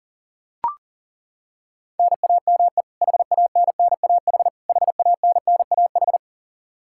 Morse code, heard as a keyed tone, about 37 wpm, 700 Hz. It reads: NAME HANNAH HANNAH
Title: Practice QSO - Exchange 1 - Beginning - 40wpm